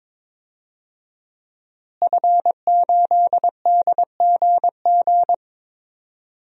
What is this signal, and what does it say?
Morse code, heard as a keyed tone, about 22 wpm, 700 Hz. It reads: F8DGG
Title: DXCC-100 Call Signs - 5 Characters - Repeated 3x - 22wpm